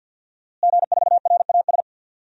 Morse code, heard as a keyed tone, about 52 wpm, 700 Hz. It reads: G4RAS
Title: DXCC-100 Call Signs - 5 Characters - Repeated 2x - 50wpm